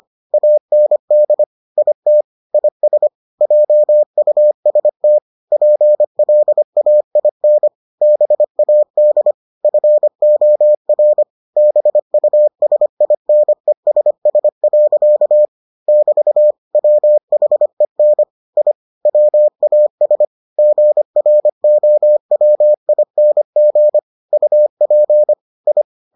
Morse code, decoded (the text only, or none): AND IT IS JUST PLAIN BAD FOR BUSINESS. <BT> WHEN I WAS GROWING UP I